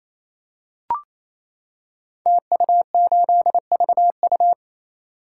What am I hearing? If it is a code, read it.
Morse code, heard as a keyed tone, about 27 wpm, 700 Hz. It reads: TU8VU
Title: Code Groups - 5 Letter:Numbers - 28wpm